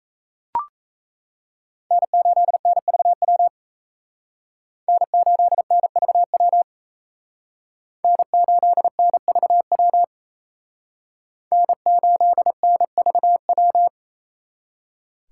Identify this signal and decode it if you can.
Morse code, heard as a keyed tone, about 34 wpm, 700 Hz. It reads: N8NVW N8NVW N8NVW N8NVW
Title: Speed Racing - USA 1x3 Call Signs 28wpm